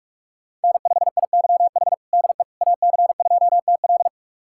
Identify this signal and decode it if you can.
Morse code, heard as a keyed tone, about 46 wpm, 700 Hz. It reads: N5IYH DE AC2TL
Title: USA Callsign DE Callsign - 5 Characters - 45wpm